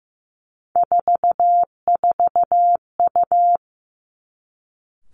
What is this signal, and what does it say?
Morse code, heard as a keyed tone, about 15 wpm, 700 Hz. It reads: 44U
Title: Three Challenging Letter-Number Combinations 15wpm